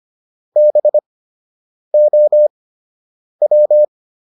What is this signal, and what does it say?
Morse code, heard as a keyed tone, about 25 wpm, spaced as for 8 wpm, 600 Hz. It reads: BOW